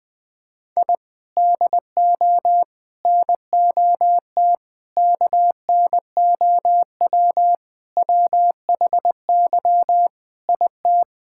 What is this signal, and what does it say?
Morse code, heard as a keyed tone, about 20 wpm, 700 Hz. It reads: I DO NOT KNOW WHY IT
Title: Sentences with 5-Word Phrases - 20wpm